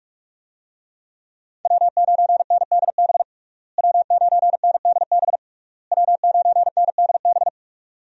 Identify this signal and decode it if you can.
Morse code, heard as a keyed tone, about 46 wpm, 700 Hz. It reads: W9NDB W9NDB W9NDB
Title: USA Call Signs - 5 Characters - Repeated 3x - 45wpm